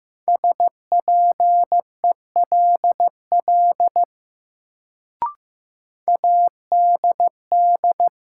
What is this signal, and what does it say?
Morse code, heard as a keyed tone, about 15 wpm, 700 Hz. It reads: SPELL ADD
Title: Sentences from Top 1000 Words - Review Part 1 - New Word Drill - 15wpm